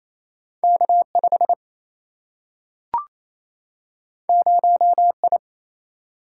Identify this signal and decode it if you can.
Morse code, heard as a keyed tone, about 28 wpm, 700 Hz. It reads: K 5 0 I
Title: Single-Letter Single-Number 28wpm